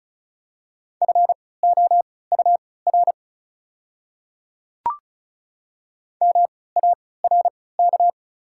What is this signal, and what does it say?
Morse code, heard as a keyed tone, about 36 wpm, spaced as for 20 wpm, 700 Hz. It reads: FOUR MARK
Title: 3x Character Spacing - Top 300 Words - 35wpm